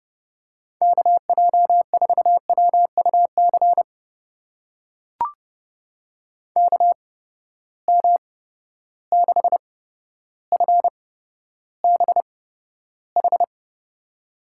Morse code, heard as a keyed tone, about 30 wpm, spaced as for 13 wpm, 700 Hz. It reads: KJ4WUC KM6FBH
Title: USA Call Signs - 6 Characters - 8x Character Spacing - 30wpm